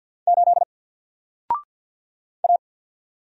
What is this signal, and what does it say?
Morse code, heard as a keyed tone, about 48 wpm, 700 Hz. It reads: 8 A
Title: Single Character (Letter:Number:Punctuation:Prosign) Alt 50wpm